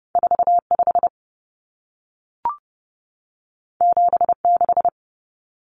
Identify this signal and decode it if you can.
Morse code, heard as a keyed tone, about 30 wpm, 700 Hz. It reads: 45 76
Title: Numbers - 2-Digit 30wpm